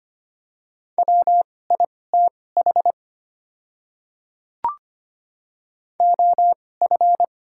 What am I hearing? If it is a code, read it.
Morse code, heard as a keyed tone, about 25 wpm, spaced as for 18 wpm, 690 Hz. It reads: WITH OF
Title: 2x Character Spacing - Top 100 Words - 25wpm